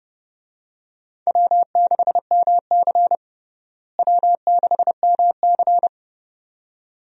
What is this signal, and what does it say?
Morse code, heard as a keyed tone, about 32 wpm, 700 Hz. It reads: W6MC W6MC
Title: USA Call Signs - 4 Characters - Repeated 2x - 30wpm